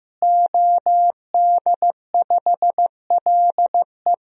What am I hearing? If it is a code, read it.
Morse code, heard as a keyed tone, about 15 wpm, 700 Hz. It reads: OD5LE
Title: DXCC-100 Call Signs - 5 Characters - Repeated 2x - 15wpm